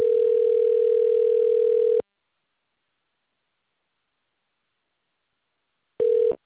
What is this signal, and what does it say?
Ringback tone (North America), 440 + 480 Hz.